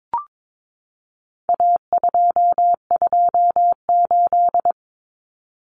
Morse code, heard as a keyed tone, about 22 wpm, 700 Hz. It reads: A228
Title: Common Cut Numbers - 4-Digit 22wpm